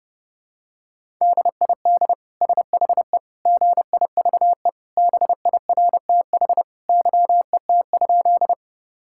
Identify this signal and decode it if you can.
Morse code, heard as a keyed tone, about 30 wpm, 700 Hz. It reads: DID SHE GIVE BIRTH YET?
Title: Sentences from Top 1000 Words - Review Part 5 - 30wpm